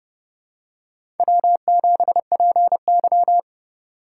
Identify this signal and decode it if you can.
Morse code, heard as a keyed tone, about 31 wpm, 700 Hz. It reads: W7PY
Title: USA Call Signs - 4 Characters - 3x Character Spacing - 30wpm